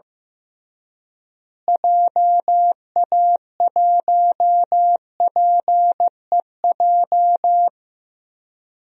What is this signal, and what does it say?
Morse code, heard as a keyed tone, about 15 wpm, 700 Hz. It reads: JA1PEJ
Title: DXCC-100 Call Signs - 6 Characters - Repeated 3x - 15wpm